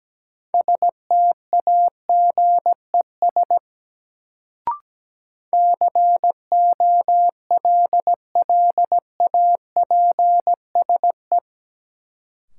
Morse code, heard as a keyed tone, about 17 wpm, 700 Hz. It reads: STAGES COLLAPSE
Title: Sets of 5 Words Encore - New Word Drill - 17wpm